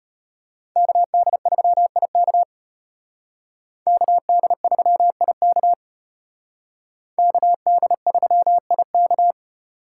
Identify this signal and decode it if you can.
Morse code, heard as a keyed tone, about 33 wpm, 700 Hz. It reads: KD3IK KD3IK KD3IK